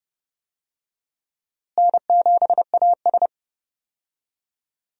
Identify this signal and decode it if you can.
Morse code, heard as a keyed tone, about 29 wpm, 700 Hz. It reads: N7AS
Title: USA Call Signs - 4 Characters - Repeated 3x - 30wpm